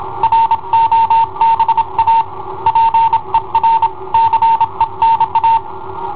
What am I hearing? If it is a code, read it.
Morse code, heard as a keyed tone, about 27 wpm, 910 Hz. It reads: ROBA PER CENA
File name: i7ohp / cassetta i7ohp B